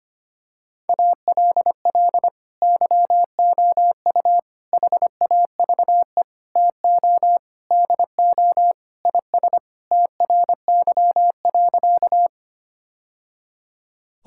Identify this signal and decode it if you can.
Morse code, heard as a keyed tone, about 25 wpm, 700 Hz. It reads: ALL YOU HAVE TO DO IS TRY.
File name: practice / 25wpm / Sentences with 3-Word Phrases - 25wpm